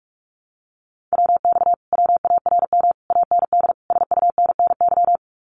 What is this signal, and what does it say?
Morse code, heard as a keyed tone, about 46 wpm, 700 Hz. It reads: WX WARM AND SUNNY
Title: Common QSO Phrases Encore 45wpm